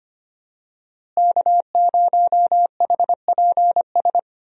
Morse code, heard as a keyed tone, about 25 wpm, 700 Hz. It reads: K0HPS
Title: USA Call Signs - 5 Characters - Repeated 3x - 25wpm